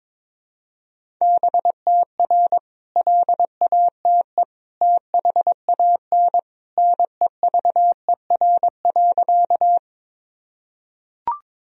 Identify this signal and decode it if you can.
Morse code, heard as a keyed tone, about 23 wpm, 700 Hz. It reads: BTR LATE THAN NEVER.
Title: Sentences from Top 400 Words and Abbreviations 22wpm